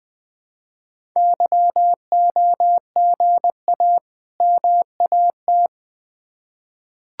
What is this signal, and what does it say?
Morse code, heard as a keyed tone, about 20 wpm, 700 Hz. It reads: YOGA MAT
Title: Difficult Words with Hints - 20wpm